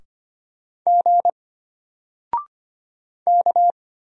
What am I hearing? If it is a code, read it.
Morse code, heard as a keyed tone, about 25 wpm, 700 Hz. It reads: G K
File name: Single-Letter-Number-25wpm